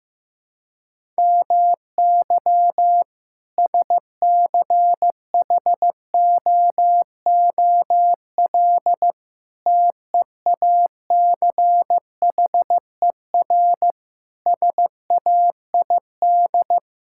Morse code, heard as a keyed tone, about 15 wpm, 700 Hz. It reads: MY SCHOOL TEACHER SAID
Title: Sentences from Top 1000 Words - Review Part 6 - 15wpm